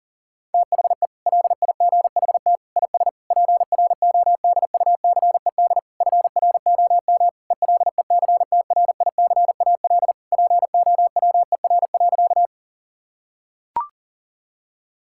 Morse code, heard as a keyed tone, about 40 wpm, 690 Hz. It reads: THE LIGHT IS PRODUCED FROM ELECTRICAL POWER.
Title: Sentences from Top 1000 Words - Review Part 6 - 40wpm